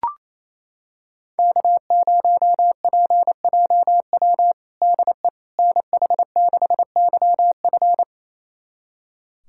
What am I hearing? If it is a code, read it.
Morse code, heard as a keyed tone, about 28 wpm, 700 Hz. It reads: K0PJW DE NH6YF
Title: USA Callsign DE Callsign - 5 Characters - 28wpm